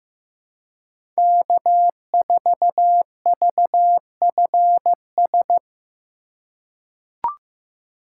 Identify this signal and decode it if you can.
Morse code, heard as a keyed tone, about 16 wpm, 700 Hz. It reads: K4VFS
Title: USA Call Signs - 5 Characters - 4x Character Spacing - 15wpm